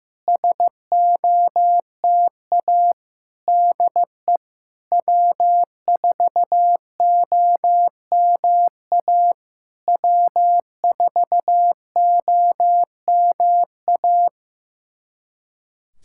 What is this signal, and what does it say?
Morse code, heard as a keyed tone, about 15 wpm, 700 Hz. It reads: SOTA DE W4OMA W4OMA
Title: SOTA QSO Practice 15wpm